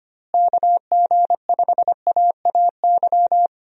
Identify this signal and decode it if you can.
Morse code, heard as a keyed tone, about 25 wpm, 700 Hz. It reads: KG5AAY
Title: USA Call Signs - 6 Characters - Repeated 3x - 25wpm